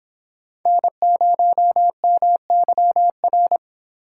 Morse code, heard as a keyed tone, about 25 wpm, 700 Hz. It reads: N0MYR